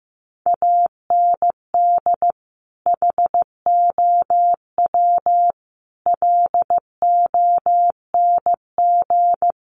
Morse code, heard as a keyed tone, about 14 wpm, 700 Hz. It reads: AND HOW LONG
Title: Sentences from Top 1500 Words 15wpm